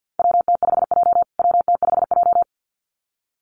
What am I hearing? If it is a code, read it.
Morse code, heard as a keyed tone, about 50 wpm, 700 Hz. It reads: WT5J WT5J
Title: USA Call Signs Repeated 2x 50wpm